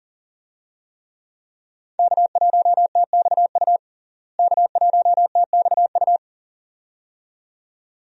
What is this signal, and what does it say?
Morse code, heard as a keyed tone, about 40 wpm, 700 Hz. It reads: K1TXU K1TXU
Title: USA Call Signs - 5 Characters - Repeated 2x - 40wpm